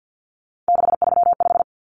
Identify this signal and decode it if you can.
Morse code, heard as a keyed tone, about 50 wpm, 700 Hz. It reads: 635
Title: Common Cut Numbers - 3-Digit 50wpm